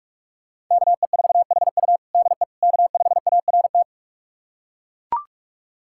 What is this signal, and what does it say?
Morse code, heard as a keyed tone, about 42 wpm, 700 Hz. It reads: KE4HU DE K5ART
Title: USA Callsign DE Callsign - 5 Characters - 45wpm